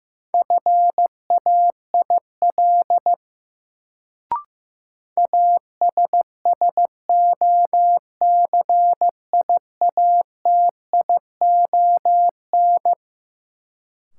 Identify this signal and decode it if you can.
Morse code, heard as a keyed tone, about 15 wpm, 700 Hz. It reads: FAIL ASSOCIATION
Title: Sentences from Top 1100 Words - New Word Drill - 15wpm